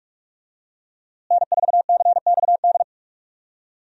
Morse code, heard as a keyed tone, about 41 wpm, 700 Hz. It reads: N4CXD
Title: USA Call Signs - 5 Characters - Repeated 2x - 45wpm